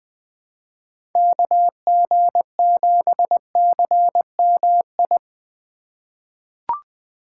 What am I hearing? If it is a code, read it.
Morse code, heard as a keyed tone, about 20 wpm, 700 Hz. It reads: KG7CMI
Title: USA Call Signs - 6 Characters - 3x Character Spacing - 20wpm